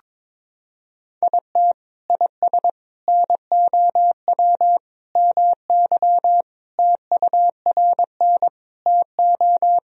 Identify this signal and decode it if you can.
Morse code, heard as a keyed tone, about 22 wpm, 700 Hz. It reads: IT IS NOW MY TURN TO
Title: Sentences from Top 200 Words and Abbreviations 22wpm